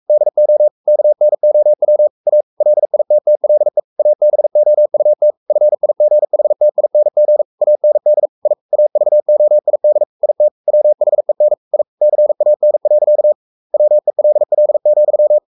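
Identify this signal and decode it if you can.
Morse code, heard as a keyed tone, about 44 wpm, 600 Hz. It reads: DO KNOW A LITTLE ABOUT FIGHTING AND I AVOID IT WHEN I CAN. WELL,
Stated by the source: Morse code at 43 wpm sending Ken Follett's novel Never in English, from CW med Gnister.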